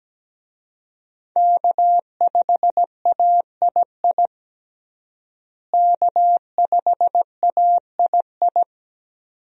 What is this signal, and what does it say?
Morse code, heard as a keyed tone, about 17 wpm, 700 Hz. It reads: K5AII K5AII
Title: USA Call Signs - 5 Characters - Repeated 2x - 17wpm